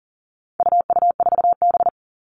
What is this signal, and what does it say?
Morse code, heard as a keyed tone, about 40 wpm, 700 Hz. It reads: UU4B
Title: All Cut Numbers - 4-Digit 40wpm